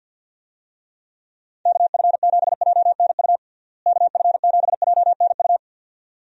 Morse code, heard as a keyed tone, about 51 wpm, 700 Hz. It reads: KF7JNU KF7JNU
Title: USA Call Signs - 6 Characters - Repeated 2x - 50wpm